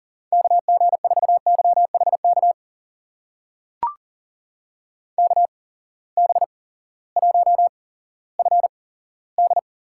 Morse code, heard as a keyed tone, about 39 wpm, spaced as for 18 wpm, 700 Hz. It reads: KG4YHK KB1FD
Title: USA Call Signs - 6 Characters - 8x Character Spacing - 40wpm